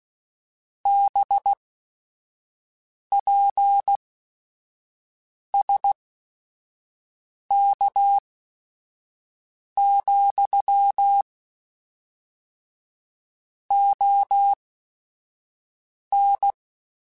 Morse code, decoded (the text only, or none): BPSK, ON